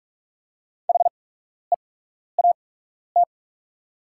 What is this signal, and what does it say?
Morse code, heard as a keyed tone, about 44 wpm, spaced as for 12 wpm, 690 Hz. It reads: HEAT
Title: 8x Character Spacing - Top 500 Words - 45wpm